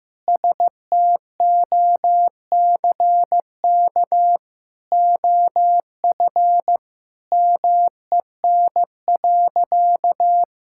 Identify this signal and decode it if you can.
Morse code, heard as a keyed tone, about 15 wpm, 700 Hz. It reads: STOCK OF MEN.
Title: Sentences from Top 1000 Words - Review Part 5 - 15wpm